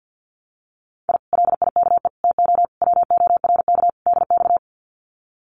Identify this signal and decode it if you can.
Morse code, heard as a keyed tone, about 50 wpm, 700 Hz. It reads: I LIKE TO WORK DX
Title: Common QSO Phrases Encore 50wpm